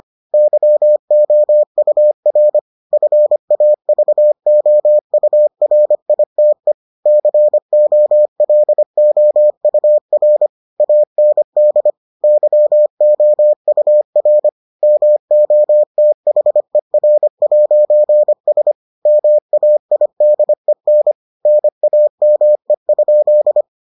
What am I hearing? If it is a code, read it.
Morse code, heard as a keyed tone, about 25 wpm, 600 Hz. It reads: YOUR FAVOURITE COLOUR AND YOUR MOTHER'S MAIDEN NAME?